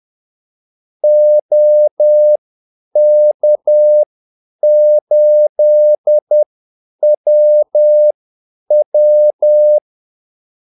Morse code, heard as a keyed tone, about 10 wpm, spaced as for 8 wpm, 600 Hz. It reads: OK8WW
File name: Callsigns-10-8wpm